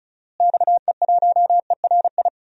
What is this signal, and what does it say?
Morse code, heard as a keyed tone, about 36 wpm, 700 Hz. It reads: XE1ERI
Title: DXCC-100 Call Signs - 6 Characters - Repeated 3x - 35wpm